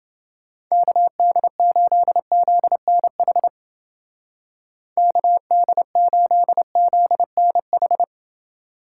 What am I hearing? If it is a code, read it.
Morse code, heard as a keyed tone, about 28 wpm, 700 Hz. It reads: KD8ZNH KD8ZNH